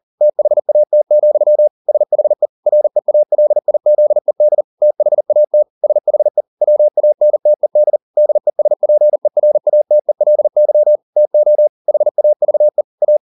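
Morse code, heard as a keyed tone, about 40 wpm, 600 Hz. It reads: THAT, SHE REALIZED THAT SHE WANTED DESPERATELY TO HAVE A
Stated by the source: Morse code at 40 wpm, practice text, CW med Gnister.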